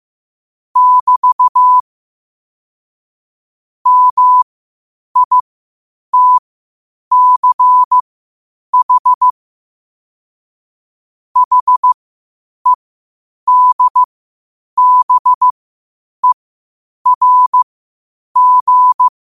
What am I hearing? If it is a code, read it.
Morse code, heard as a keyed tone, about 15 wpm, spaced as for 8 wpm, 1000 Hz. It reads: <BT> MITCH HEDBERG
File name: qotdfiles / 8wpm